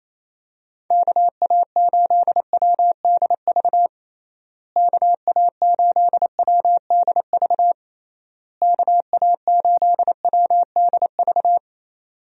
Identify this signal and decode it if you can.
Morse code, heard as a keyed tone, about 27 wpm, 700 Hz. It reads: KA8WDV KA8WDV KA8WDV
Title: USA Call Signs - 6 Characters - Repeated 3x - 28wpm